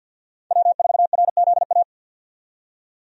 Morse code, heard as a keyed tone, about 52 wpm, 700 Hz. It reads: W4RZA
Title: USA Call Signs - 5 Characters - 50wpm